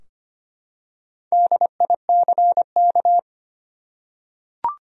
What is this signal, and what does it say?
Morse code, heard as a keyed tone, about 25 wpm, 700 Hz. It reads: DICK
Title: USA Common Names - 25wpm